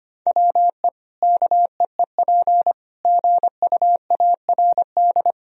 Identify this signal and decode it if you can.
Morse code, heard as a keyed tone, about 25 wpm, 700 Hz. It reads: WE KEEP GUARD
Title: Sentences from Top 1000 Words - Review Part 4 - 25wpm